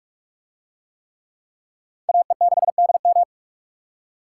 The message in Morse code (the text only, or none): AE6DM